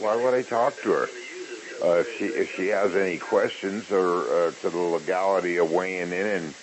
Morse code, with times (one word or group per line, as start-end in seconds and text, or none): none